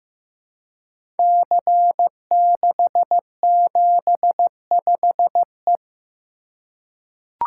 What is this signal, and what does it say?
Morse code, heard as a keyed tone, about 15 wpm, 700 Hz. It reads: C675E
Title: Code Groups - 5 Letter:Numbers - 15wpm